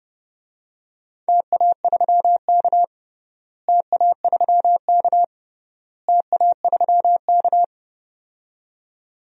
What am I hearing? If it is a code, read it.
Morse code, heard as a keyed tone, about 30 wpm, 700 Hz. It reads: TA3K TA3K TA3K
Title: DXCC-100 Call Signs - 4 Characters - Repeated 3x - 30wpm